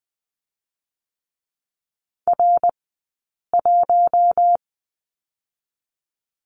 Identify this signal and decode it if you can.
Morse code, heard as a keyed tone, about 20 wpm, 700 Hz. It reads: R 1
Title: 080 Lesson 14d - Number 2 - Two Characters 20wpm